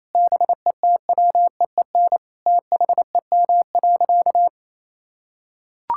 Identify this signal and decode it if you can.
Morse code, heard as a keyed tone, about 27 wpm, 700 Hz. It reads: BETWEEN THEM.
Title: Sentences from Top 1000 Words - Review Part 6 - 28wpm